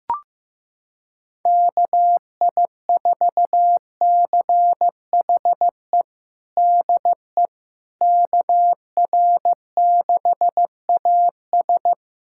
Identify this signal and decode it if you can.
Morse code, heard as a keyed tone, about 15 wpm, 700 Hz. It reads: KI4CHE DE KR6AS
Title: USA Callsign DE Callsign 15wpm